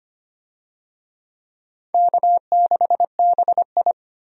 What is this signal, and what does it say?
Morse code, heard as a keyed tone, about 25 wpm, 700 Hz. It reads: K6BI
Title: USA Call Signs - 4 Characters - 25wpm